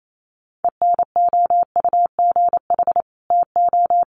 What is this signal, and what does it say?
Morse code, heard as a keyed tone, about 28 wpm, 700 Hz. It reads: ENOUGH TO
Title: Sets of 2 Words 28wpm